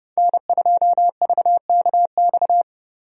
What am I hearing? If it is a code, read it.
Morse code, heard as a keyed tone, about 30 wpm, 700 Hz. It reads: N2VKX
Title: USA Call Signs - 5 Characters - Repeated 2x - 30wpm